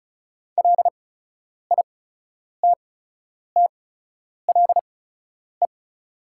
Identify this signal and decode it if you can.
Morse code, heard as a keyed tone, about 34 wpm, spaced as for 10 wpm, 700 Hz. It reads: LITTLE
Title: 8x Character Spacing - Top 300 Words - 35wpm